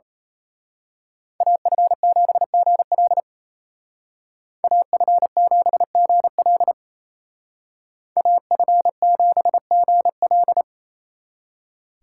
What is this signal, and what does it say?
Morse code, heard as a keyed tone, about 34 wpm, 700 Hz. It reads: AF7GL AF7GL AF7GL
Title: Speed Racing - USA 2x2 Call Signs 28wpm